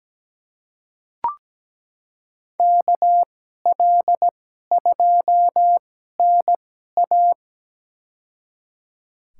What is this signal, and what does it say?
Morse code, heard as a keyed tone, about 17 wpm, spaced as for 12 wpm, 700 Hz. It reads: KL2NA
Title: USA Call Signs - 2x Character Spacing - 17wpm